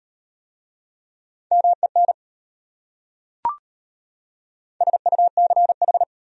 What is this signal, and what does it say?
Morse code, heard as a keyed tone, about 33 wpm, 700 Hz. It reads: MEN SUCH